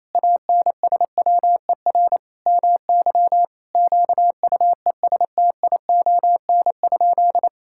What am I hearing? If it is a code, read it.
Morse code, heard as a keyed tone, about 28 wpm, 700 Hz. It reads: ANSWER MY QUESTION?
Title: Sentences from Top 1000 Words - Review Part 5 - 28wpm